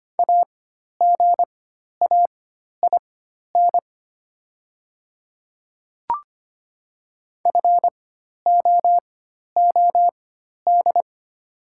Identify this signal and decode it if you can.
Morse code, heard as a keyed tone, about 25 wpm, spaced as for 12 wpm, 700 Hz. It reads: AGAIN FOOD
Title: 4x Character Spacing - Top 300 Words - 25wpm